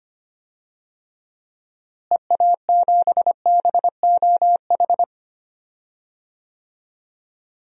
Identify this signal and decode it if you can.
Morse code, heard as a keyed tone, about 25 wpm, 700 Hz. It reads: EA7BOH
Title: DXCC-100 Call Signs - 6 Characters - Repeated 3x - 25wpm